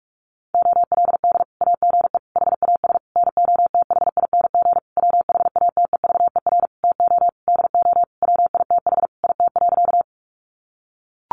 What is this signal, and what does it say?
Morse code, heard as a keyed tone, about 44 wpm, 700 Hz. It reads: OLD AGE HAS NOTHING WHATEVER TO DO WITH IT.
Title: Sentences from Top 1100 Words 45wpm